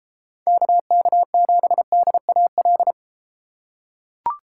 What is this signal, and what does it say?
Morse code, heard as a keyed tone, about 32 wpm, 700 Hz. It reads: KK7DAL